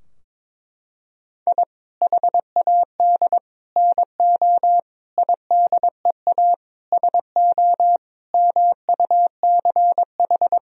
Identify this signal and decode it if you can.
Morse code, heard as a keyed tone, about 22 wpm, 700 Hz. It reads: I HAD NO IDEA SO MUCH
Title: Sentences from Top 1000 Words - Review Part 4 - 22wpm